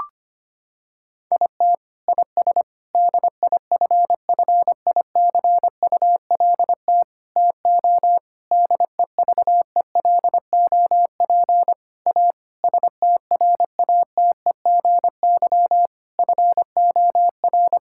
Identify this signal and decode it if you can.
Morse code, heard as a keyed tone, about 25 wpm, 700 Hz. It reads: IT IS DIFFICULT TO DEVELOP A STRATEGY FOR
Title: Sentences from Top 1000 Words - Review Part 5 - 25wpm